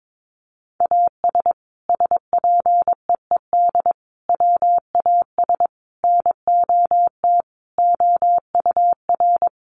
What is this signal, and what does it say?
Morse code, heard as a keyed tone, about 22 wpm, 700 Hz. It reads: AS SPEED WAS NOT OUR
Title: Sentences from Top 800 Words 22wpm